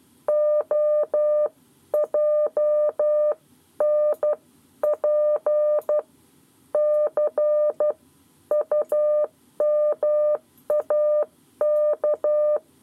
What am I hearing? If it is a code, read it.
Morse code, heard as a keyed tone, about 12 wpm, spaced as for 10 wpm, 600 Hz. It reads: OJNP C UMAK
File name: 7 Lezione cw 1